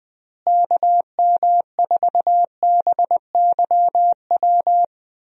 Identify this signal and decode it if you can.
Morse code, heard as a keyed tone, about 20 wpm, 700 Hz. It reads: KM4BYW
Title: USA Call Signs - 6 Characters - Repeated 3x - 20wpm